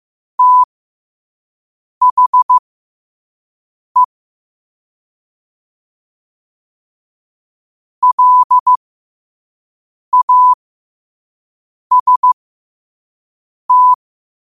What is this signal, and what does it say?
Morse code, heard as a keyed tone, about 15 wpm, spaced as for 5 wpm, 1000 Hz. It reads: THE LAST
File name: qotdfiles / 5wpm